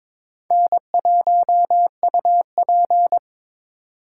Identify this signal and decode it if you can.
Morse code, heard as a keyed tone, about 22 wpm, 700 Hz. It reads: N1UP
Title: USA Call Signs - 4 Characters - 3x Character Spacing - 22wpm